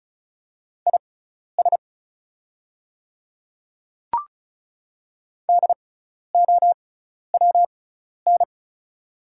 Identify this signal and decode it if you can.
Morse code, heard as a keyed tone, about 36 wpm, spaced as for 12 wpm, 700 Hz. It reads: IS DOWN
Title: 6x Character Spacing - Top 100 Words - 35wpm